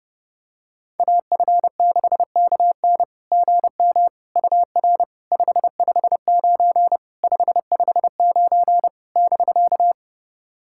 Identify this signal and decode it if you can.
Morse code, heard as a keyed tone, about 30 wpm, 690 Hz. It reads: AF6KN GM UR 559 559 <BK>
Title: SOTA QSO Practice 30wpm